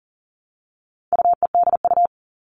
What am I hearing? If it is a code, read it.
Morse code, heard as a keyed tone, about 40 wpm, 700 Hz. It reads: UEDU
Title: All Cut Numbers - 4-Digit 40wpm